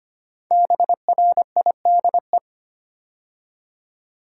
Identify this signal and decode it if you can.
Morse code, heard as a keyed tone, about 25 wpm, 700 Hz. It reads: BRIDE
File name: 5 Letter Words - First 3 Letters Given - 25wpm